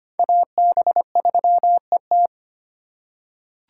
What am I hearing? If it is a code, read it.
Morse code, heard as a keyed tone, about 25 wpm, 700 Hz. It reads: AB3ET
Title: USA Call Signs - 5 Characters - 3x Character Spacing - 25wpm